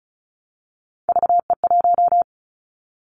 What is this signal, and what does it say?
Morse code, heard as a keyed tone, about 36 wpm, 690 Hz. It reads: VE1
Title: Callsign Prefixes for Top 25 DXCC - 35wpm